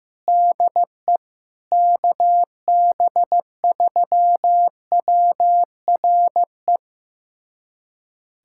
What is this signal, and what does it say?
Morse code, heard as a keyed tone, about 16 wpm, 700 Hz. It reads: DE KB3WRE
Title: USA Callsign DE Callsign 15wpm